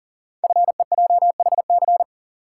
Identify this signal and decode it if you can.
Morse code, heard as a keyed tone, about 38 wpm, 690 Hz. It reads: FEJHC
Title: Code Groups - 5 Letters - 40wpm